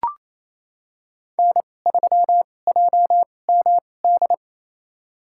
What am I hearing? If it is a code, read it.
Morse code, heard as a keyed tone, about 27 wpm, spaced as for 20 wpm, 700 Hz. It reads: N3JMD
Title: USA Call Signs - 5 Characters - 2x Character Spacing - 28wpm